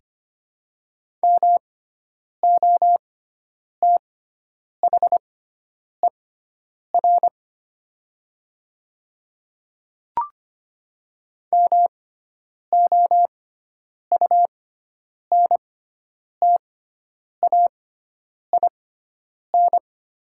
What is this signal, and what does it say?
Morse code, heard as a keyed tone, about 25 wpm, spaced as for 9 wpm, 700 Hz. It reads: MOTHER MOUNTAIN
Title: 6x Character Spacing - Top 300 Words - 25wpm